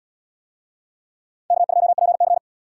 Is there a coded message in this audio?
Yes: Morse code from 1.5 to 2.4 s, heard as a keyed tone, about 73 wpm, 700 Hz. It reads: K2PC